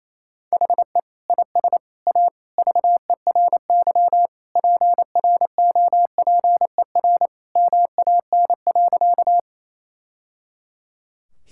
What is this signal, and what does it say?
Morse code, heard as a keyed tone, about 28 wpm, 700 Hz. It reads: HE IS A VERY PROPER MAN.
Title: Sentences from Top 1000 Words - Review Part 5 - 28wpm